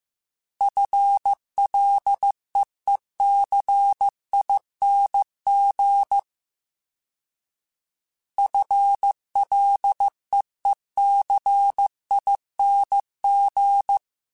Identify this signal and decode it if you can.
Morse code, heard as a keyed tone, about 15 wpm, 800 Hz. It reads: FLEECING FLEECING
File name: FL / FL mp3s / FL Prefixes 3x2 @13